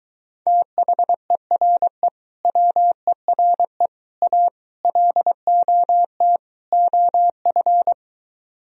Morse code, heard as a keyed tone, about 23 wpm, 700 Hz. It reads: THERE WERE A LOT OF